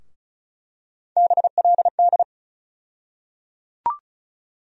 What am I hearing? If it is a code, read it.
Morse code, heard as a keyed tone, about 36 wpm, 700 Hz. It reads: BLD
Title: USA Call Sign Postfixes - 3 Letters 35wpm